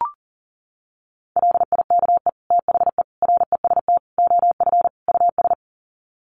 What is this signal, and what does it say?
Morse code, heard as a keyed tone, about 40 wpm, 700 Hz. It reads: LIKE THE REST OF US
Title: Sets of 5 Words Encore2 40wpm